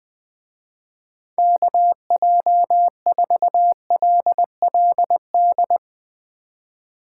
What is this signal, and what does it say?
Morse code, heard as a keyed tone, about 20 wpm, 700 Hz. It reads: KJ4LLD